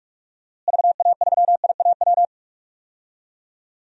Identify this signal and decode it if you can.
Morse code, heard as a keyed tone, about 46 wpm, 700 Hz. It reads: VA3IAW
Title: DXCC-100 Call Signs - 6 Characters - Repeated 3x - 45wpm